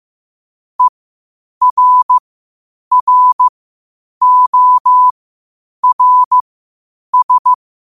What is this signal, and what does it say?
Morse code, heard as a keyed tone, about 15 wpm, spaced as for 8 wpm, 1000 Hz. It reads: ERRORS
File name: qotdfiles / 8wpm